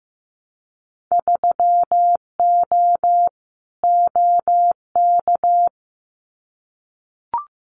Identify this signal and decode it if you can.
Morse code, heard as a keyed tone, about 15 wpm, 700 Hz. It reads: 3O OK
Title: Winter Field Day Exchanges Encore 15wpm